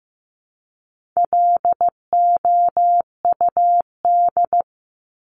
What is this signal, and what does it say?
Morse code, heard as a keyed tone, about 15 wpm, 700 Hz. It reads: LOUD
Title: Top 1751-2000 Words 15wpm